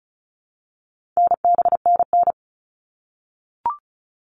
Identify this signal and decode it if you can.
Morse code, heard as a keyed tone, about 34 wpm, 700 Hz. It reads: NBNN
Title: All Cut Numbers - 4-Digit 35wpm